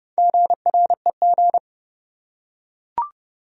Morse code, heard as a keyed tone, about 31 wpm, 700 Hz. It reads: GREG
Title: USA Common Names - Max 5 Letters - 30wpm